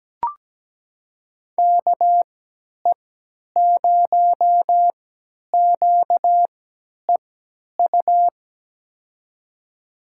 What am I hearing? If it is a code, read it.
Morse code, heard as a keyed tone, about 17 wpm, spaced as for 10 wpm, 700 Hz. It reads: KE0QEU
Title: USA Call Signs - 6 Characters - 3x Character Spacing - 17wpm